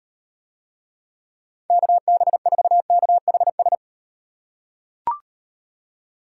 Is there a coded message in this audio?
Yes: Morse code from 1.7 to 3.8 s, heard as a keyed tone, about 36 wpm, 700 Hz. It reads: KB4KHS